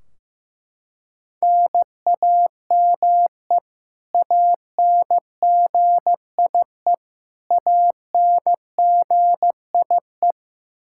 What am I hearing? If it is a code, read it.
Morse code, heard as a keyed tone, about 15 wpm, 700 Hz. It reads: NAME ANGIE ANGIE
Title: Practice QSO - Exchange 1 - Beginning - 15wpm